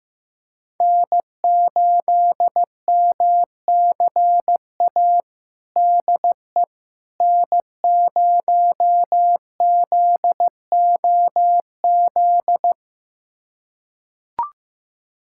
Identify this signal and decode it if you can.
Morse code, heard as a keyed tone, about 15 wpm, 700 Hz. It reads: N8MCA DE N0ZOZ
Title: USA Callsign DE Callsign - 5 Characters - 15wpm